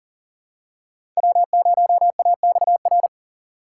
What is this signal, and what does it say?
Morse code, heard as a keyed tone, about 40 wpm, 700 Hz. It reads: W0AXR
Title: USA Call Signs - 3x Character Spacing - 40wpm